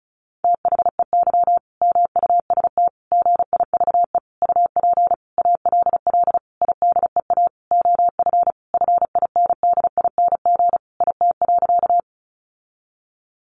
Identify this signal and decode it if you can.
Morse code, heard as a keyed tone, about 34 wpm, 700 Hz. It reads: THEY MUST GIVE UP ALL IDEA OF FINDING IT.
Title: Sentences from Top 1100 Words 35wpm